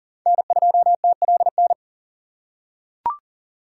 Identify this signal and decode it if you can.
Morse code, heard as a keyed tone, about 40 wpm, 700 Hz. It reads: N2TLN
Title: USA Call Signs - 5 Characters - 3x Character Spacing - 40wpm